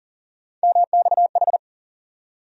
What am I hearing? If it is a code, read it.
Morse code, heard as a keyed tone, about 40 wpm, 700 Hz. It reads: MXH
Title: USA Call Sign Postfixes - 3 Letters 40wpm